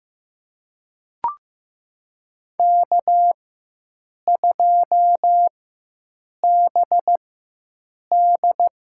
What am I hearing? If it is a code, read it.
Morse code, heard as a keyed tone, about 15 wpm, spaced as for 7 wpm, 700 Hz. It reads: K2BD
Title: USA Call Signs - 4 Characters - 4x Character Spacing - 15wpm